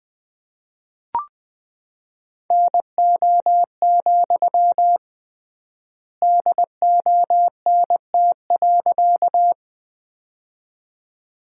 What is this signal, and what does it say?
Morse code, heard as a keyed tone, about 20 wpm, 700 Hz. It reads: NO, DONT.
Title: 3x Word-Spacing - Sentences from Top 300 Words - 20wpm